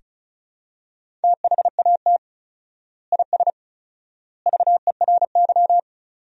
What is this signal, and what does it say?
Morse code, heard as a keyed tone, about 34 wpm, 700 Hz. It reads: THAT IS VERY
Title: 4x Word-Spacing - Sentences from Top 400 Words - 35wpm